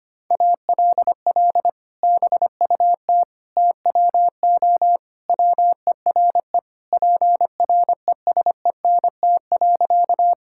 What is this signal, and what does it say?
Morse code, heard as a keyed tone, about 25 wpm, 700 Hz. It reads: ALL BUT TWO WERE PRESENT.
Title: Sentences from Top 1000 Words - Review Part 5 - 25wpm